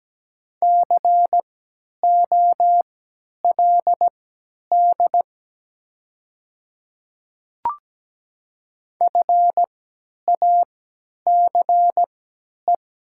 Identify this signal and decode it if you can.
Morse code, heard as a keyed tone, about 17 wpm, spaced as for 10 wpm, 700 Hz. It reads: COLD FACE
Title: 3x Character Spacing - Top 500 Words - 17wpm